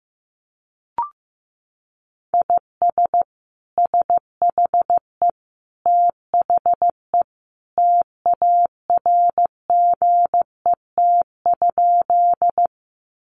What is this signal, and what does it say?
Morse code, heard as a keyed tone, about 14 wpm, 700 Hz. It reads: IS SHE THE TARGET?
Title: Sentences from Top 1200 Words 15wpm